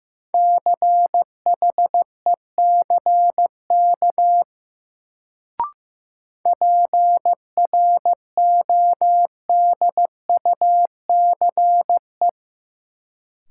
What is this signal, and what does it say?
Morse code, heard as a keyed tone, about 15 wpm, 700 Hz. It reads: CHECK PRODUCE
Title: Sentences from Top 1000 Words - Review Part 3 - New Word Drill - 15wpm